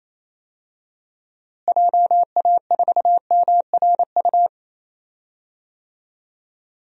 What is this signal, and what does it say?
Morse code, heard as a keyed tone, about 27 wpm, 700 Hz. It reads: JA4MRU
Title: DXCC-100 Call Signs - 6 Characters - Repeated 2x - 28wpm